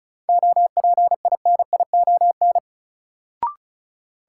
Morse code, heard as a keyed tone, about 36 wpm, 700 Hz. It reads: OPINION
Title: Sentences from Top 1200 Words - New Word Drill - 35wpm